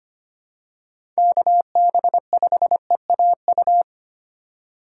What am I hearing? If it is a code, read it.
Morse code, heard as a keyed tone, about 25 wpm, 700 Hz. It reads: KB5EAU
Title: USA Call Signs - 6 Characters - 3x Character Spacing - 25wpm